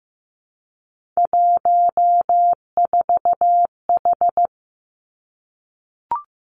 Morse code, heard as a keyed tone, about 15 wpm, 700 Hz. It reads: 14H
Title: Three Challenging Letter-Number Combinations 15wpm